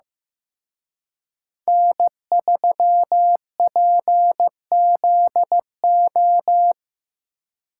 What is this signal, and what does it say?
Morse code, heard as a keyed tone, about 18 wpm, 700 Hz. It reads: N3PZO
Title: Speed Racing - USA Call Signs 15wpm